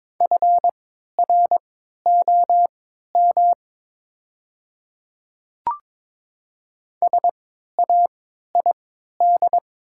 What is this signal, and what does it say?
Morse code, heard as a keyed tone, about 22 wpm, spaced as for 12 wpm, 690 Hz. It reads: FROM SAID
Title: 3x Character Spacing - Top 200 Words - 22wpm